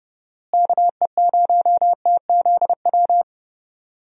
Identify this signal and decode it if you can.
Morse code, heard as a keyed tone, about 30 wpm, 700 Hz. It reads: KE0TZW
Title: USA Call Signs - 6 Characters - Repeated 3x - 30wpm